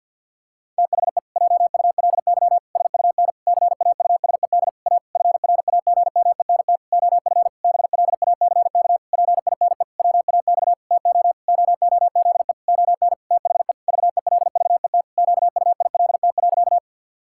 Morse code, heard as a keyed tone, about 50 wpm, 700 Hz. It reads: THE JULY SUN CAUSED A FRAGMENT OF BLACK PINE WAX TO OOZE ON THE VELVET QUILT.
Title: Pangram Challenge v2 - 50wpm